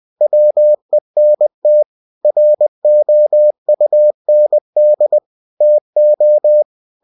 Morse code, heard as a keyed tone, about 20 wpm, 600 Hz. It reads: WENT ROUND TO
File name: cw-books / winnie-the-pooh / winnie-20wpm-page-0004